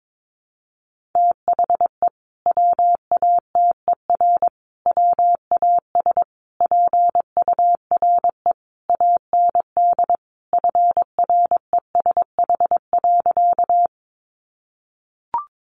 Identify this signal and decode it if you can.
Morse code, heard as a keyed tone, about 22 wpm, 700 Hz. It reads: THE WATER WAS PURE AND FRESH.
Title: Sentences from Top 1500 Words 22wpm